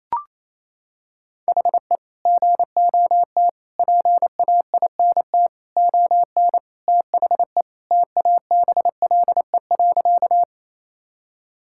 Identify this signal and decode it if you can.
Morse code, heard as a keyed tone, about 28 wpm, 700 Hz. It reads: HE GOT PAINT ON THE TABLE.
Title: Sentences from Top 1000 Words - Review Part 6 - 28wpm